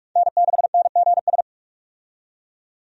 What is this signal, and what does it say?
Morse code, heard as a keyed tone, about 41 wpm, 700 Hz. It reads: N6NGS
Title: USA Call Signs - 5 Characters - 45wpm